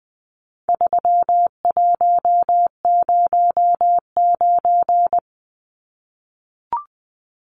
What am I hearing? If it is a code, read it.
Morse code, heard as a keyed tone, about 20 wpm, 700 Hz. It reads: 3109
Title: Numbers - 4-Digit 20wpm